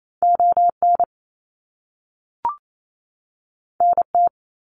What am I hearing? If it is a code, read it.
Morse code, heard as a keyed tone, about 28 wpm, 700 Hz. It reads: ON NT
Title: Canadian Provinces and Territories 28wpm